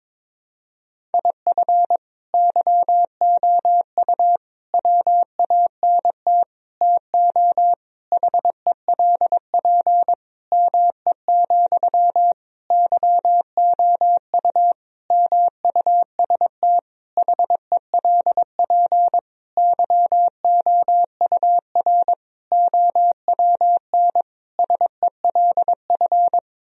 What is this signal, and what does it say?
Morse code, heard as a keyed tone, about 22 wpm, 700 Hz. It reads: IF YOU WANT TO HELP ME, YOU MUST HELP YOUR OWN SELF
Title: Sentences from Top 1000 Words - Review Part 1 - 22wpm